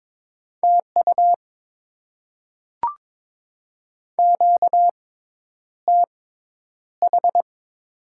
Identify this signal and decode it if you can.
Morse code, heard as a keyed tone, about 22 wpm, spaced as for 10 wpm, 700 Hz. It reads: TU QTH
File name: Top 20 QSO Elements - 6x Character Spacing - 22wpm